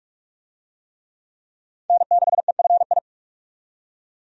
Morse code, heard as a keyed tone, about 46 wpm, 700 Hz. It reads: N6EFI
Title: USA Call Signs - 5 Characters - 45wpm